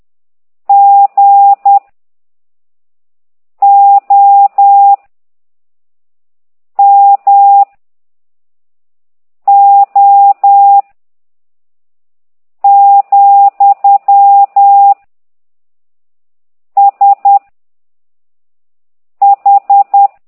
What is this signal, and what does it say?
Morse code, heard as a keyed tone, about 10 wpm, 800 Hz. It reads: G O M O , S H